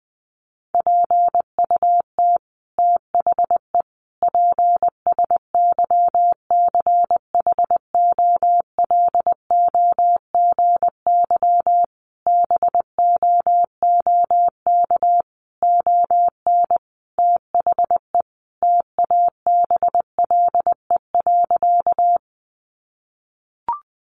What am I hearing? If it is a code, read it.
Morse code, heard as a keyed tone, about 20 wpm, 700 Hz. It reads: PUT THE PSYCHOLOGY BOOK ON THE TABLE.
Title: Sentences from Top 1400 Words 20wpm